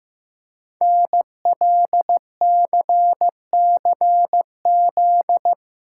Morse code, heard as a keyed tone, about 15 wpm, 700 Hz. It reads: NLCCZ
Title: Code Groups - 5 Letters - 15wpm